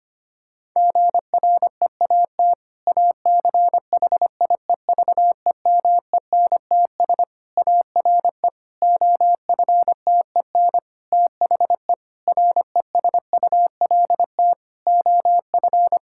Morse code, heard as a keyed tone, about 25 wpm, 700 Hz. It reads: GREAT ACHIEVEMENTS ARE OFTEN THE RESULT OF
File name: practice / 25wpm / Difficult Sentences with Hints - 25wpm